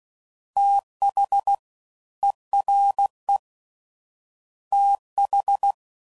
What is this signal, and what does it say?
Morse code, heard as a keyed tone, about 16 wpm, 800 Hz. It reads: TH ERE TH
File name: Prefixes / FLs & THs / Fast TH